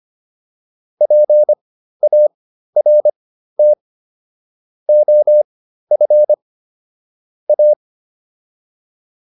Morse code, heard as a keyed tone, about 25 wpm, spaced as for 13 wpm, 600 Hz. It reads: PART OF A